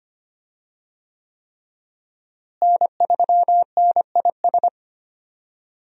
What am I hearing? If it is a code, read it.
Morse code, heard as a keyed tone, about 25 wpm, 700 Hz. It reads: N3NIS
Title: USA Call Signs - 5 Characters - Repeated 3x - 25wpm